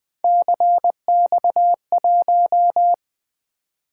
Morse code, heard as a keyed tone, about 21 wpm, 700 Hz. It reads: CX1
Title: Three Letter-Number Combinations - 20wpm